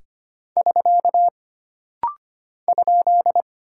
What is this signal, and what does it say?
Morse code, heard as a keyed tone, about 25 wpm, 700 Hz. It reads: <SK> ?
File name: Single Punctuation-Prosign - Rapid-Fire - 25wpm